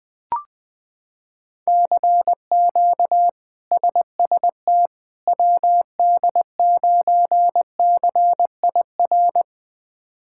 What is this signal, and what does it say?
Morse code, heard as a keyed tone, about 21 wpm, 700 Hz. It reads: CQ SST WD9CIR
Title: K1USN SST QSO Practice - 20wpm